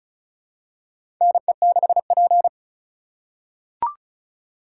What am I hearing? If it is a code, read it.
Morse code, heard as a keyed tone, about 34 wpm, 690 Hz. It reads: NE6P
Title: USA Call Signs - 3x Character Spacing - 35wpm